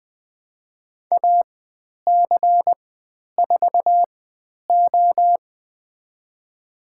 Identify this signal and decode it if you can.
Morse code, heard as a keyed tone, about 20 wpm, spaced as for 10 wpm, 700 Hz. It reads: AC4O
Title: 102 Lesson 17g - Letter u - Call Signs 20wpm